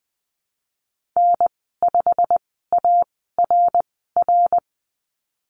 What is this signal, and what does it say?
Morse code, heard as a keyed tone, about 20 wpm, spaced as for 14 wpm, 700 Hz. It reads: N5ARR
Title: 088 Lesson 15g - Number 5 - Call Signs 20wpm